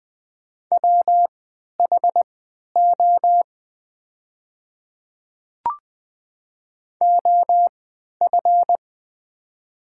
Morse code, heard as a keyed tone, about 20 wpm, spaced as for 11 wpm, 700 Hz. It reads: WHO OF
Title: 3x Character Spacing - Top 100 Words - 20wpm